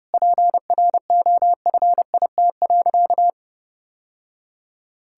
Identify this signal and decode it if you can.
Morse code, heard as a keyed tone, about 32 wpm, 700 Hz. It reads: PROFIT.
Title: Sentences from Top 1000 Words - Review Part 5 - 30wpm